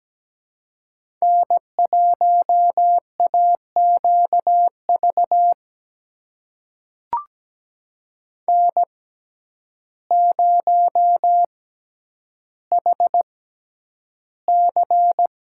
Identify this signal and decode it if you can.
Morse code, heard as a keyed tone, about 17 wpm, spaced as for 9 wpm, 700 Hz. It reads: N1AQV N0HC
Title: USA Call Signs - 5 Characters - 6x Character Spacing - 17wpm